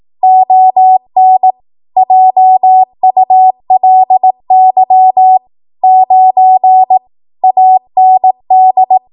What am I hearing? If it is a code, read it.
Morse code, heard as a keyed tone, about 18 wpm, 750 Hz. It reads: ON JULY 9 AND
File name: Archive / 18 WPM / 18WPM